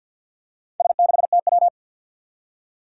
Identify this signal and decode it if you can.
Morse code, heard as a keyed tone, about 53 wpm, 700 Hz. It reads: S6TW
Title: Four Challenging Letter-Number Combinations 50wpm